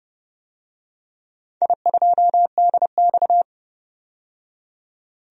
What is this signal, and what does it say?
Morse code, heard as a keyed tone, about 30 wpm, 700 Hz. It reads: I2DX
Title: DXCC-100 Call Signs - 4 Characters - 30wpm